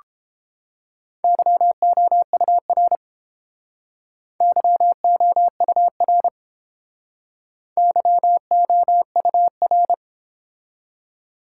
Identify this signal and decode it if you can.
Morse code, heard as a keyed tone, about 29 wpm, 700 Hz. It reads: YOUR YOUR YOUR